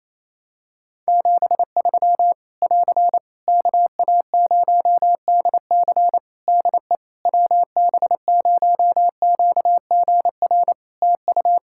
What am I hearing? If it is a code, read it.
Morse code, heard as a keyed tone, about 28 wpm, 700 Hz. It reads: 73 <AR> KA0DC DE WB0QGR TU
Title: Practice QSO - Exchange 4 - Ending - 28wpm